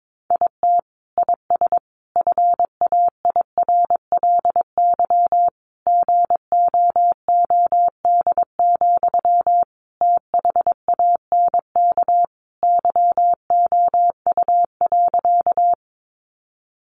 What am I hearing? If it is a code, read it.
Morse code, heard as a keyed tone, about 22 wpm, 700 Hz. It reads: IT IS FAIRLY GOOD, THANK YOU.
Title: Sentences from Top 1100 Words 22wpm